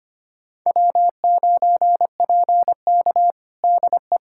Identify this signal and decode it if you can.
Morse code, heard as a keyed tone, about 25 wpm, 700 Hz. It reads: W9PK DE
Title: USA Callsign DE Callsign - 4 Characters - 25wpm